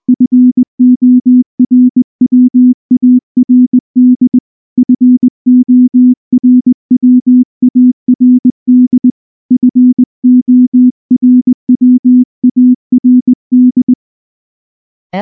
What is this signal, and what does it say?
Morse code, heard as a keyed tone, about 19 wpm, 250 Hz. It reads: FORWARD FORWARD FORWARD